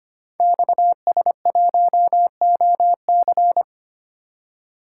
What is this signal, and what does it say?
Morse code, heard as a keyed tone, about 25 wpm, 700 Hz. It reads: XS1OC
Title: Code Groups - 5 Letter:Numbers - 25wpm